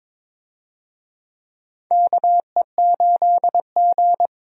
Morse code, heard as a keyed tone, about 22 wpm, 700 Hz. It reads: KE8G
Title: USA Call Signs - 4 Characters - 22wpm